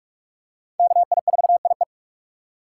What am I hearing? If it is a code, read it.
Morse code, heard as a keyed tone, about 42 wpm, 700 Hz. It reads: KI4IE
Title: USA Call Signs - 5 Characters - 4x Character Spacing - 45wpm